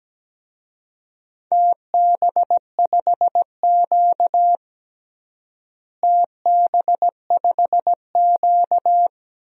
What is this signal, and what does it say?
Morse code, heard as a keyed tone, about 17 wpm, 700 Hz. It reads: TB5Q TB5Q
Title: DXCC-100 Call Signs - 4 Characters - Repeated 3x - 17wpm